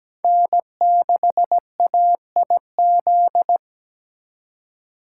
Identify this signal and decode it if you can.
Morse code, heard as a keyed tone, about 20 wpm, 700 Hz. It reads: N6AIZ
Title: Speed Racing - USA 1x3 Call Signs 17wpm